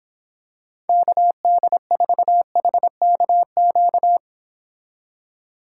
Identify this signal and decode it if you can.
Morse code, heard as a keyed tone, about 26 wpm, 700 Hz. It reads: KD4HKQ